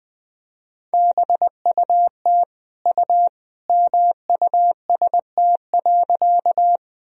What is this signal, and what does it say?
Morse code, heard as a keyed tone, about 20 wpm, 700 Hz. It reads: BUT U MUST.
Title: Sentences from Top 200 Words and Abbreviations 20wpm